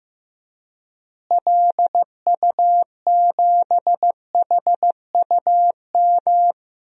Morse code, heard as a keyed tone, about 15 wpm, 700 Hz. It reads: LU7HUM
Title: DXCC-100 Call Signs - 6 Characters - Repeated 2x - 15wpm